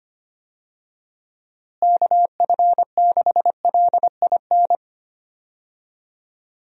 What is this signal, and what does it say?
Morse code, heard as a keyed tone, about 25 wpm, 700 Hz. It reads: KF6LIN
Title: USA Call Signs - 6 Characters - Repeated 2x - 25wpm